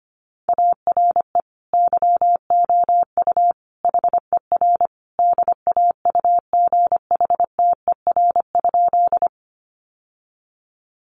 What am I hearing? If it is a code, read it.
Morse code, heard as a keyed tone, about 25 wpm, 700 Hz. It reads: ARE YOU HER DAUGHTER?
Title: Sentences from Top 1500 Words 25wpm